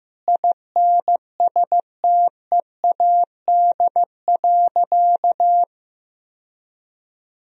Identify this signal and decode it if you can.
Morse code, heard as a keyed tone, about 15 wpm, 700 Hz. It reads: INSTEAD.
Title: Sentences from Top 1000 Words - Review Part 4 - 15wpm